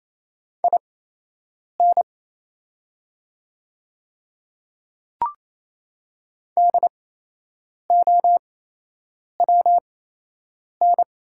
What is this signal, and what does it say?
Morse code, heard as a keyed tone, about 28 wpm, spaced as for 8 wpm, 700 Hz. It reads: IN DOWN
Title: 8x Character Spacing - Top 100 Words - 28wpm